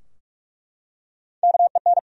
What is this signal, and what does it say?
Morse code, heard as a keyed tone, about 42 wpm, 700 Hz. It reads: KEN
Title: USA Common Names - Max 3 Letters - 45wpm